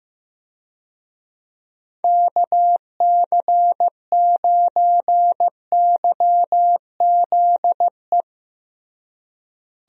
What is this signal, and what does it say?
Morse code, heard as a keyed tone, about 15 wpm, 700 Hz. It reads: KC9YZE
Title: USA Call Signs - 6 Characters - Repeated 3x - 15wpm